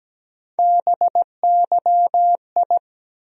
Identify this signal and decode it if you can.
Morse code, heard as a keyed tone, about 17 wpm, 700 Hz. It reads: BYI
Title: USA Call Sign Postfixes - 3 Letters 17wpm